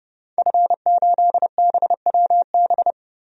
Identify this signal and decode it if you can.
Morse code, heard as a keyed tone, about 30 wpm, 700 Hz. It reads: F8BWB
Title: DXCC-100 Call Signs - 5 Characters - 30wpm